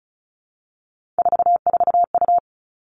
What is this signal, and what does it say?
Morse code, heard as a keyed tone, about 34 wpm, 700 Hz. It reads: 44U
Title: Three Challenging Letter-Number Combinations 35wpm